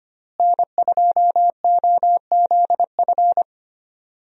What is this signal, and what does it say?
Morse code, heard as a keyed tone, about 25 wpm, 700 Hz. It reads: N2OZF
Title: USA Call Signs - 5 Characters - Repeated 2x - 25wpm